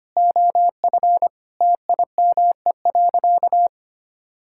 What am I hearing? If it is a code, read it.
Morse code, heard as a keyed tone, about 25 wpm, 700 Hz. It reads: OF TIME.
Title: Sentences from Top 1000 Words - Review Part 5 - 25wpm